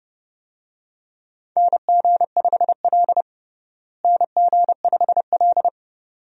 Morse code, heard as a keyed tone, about 29 wpm, 700 Hz. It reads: NG5L NG5L
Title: USA Call Signs - 4 Characters - Repeated 2x - 30wpm